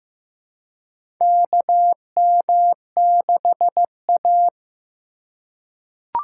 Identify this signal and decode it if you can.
Morse code, heard as a keyed tone, about 15 wpm, 700 Hz. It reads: KM6A
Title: USA Call Signs - 2x Character Spacing - 15wpm